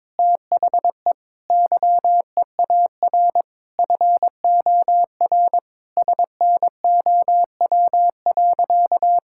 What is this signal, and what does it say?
Morse code, heard as a keyed tone, about 22 wpm, 700 Hz. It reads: THE YEAR FOR SNOW.
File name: practice / 22wpm / Sentences with 4-Word Phrases - 22wpm